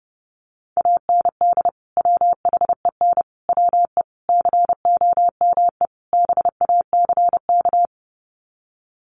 Morse code, heard as a keyed tone, about 30 wpm, 700 Hz. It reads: AND WHEN WE COME BACK
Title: Sets of 5 Words 30wpm